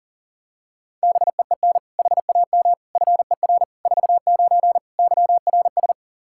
Morse code, heard as a keyed tone, about 40 wpm, 700 Hz. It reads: BEEN HAM FER 49 YRS
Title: Practice QSO - Exchange 3 - More Info - 40wpm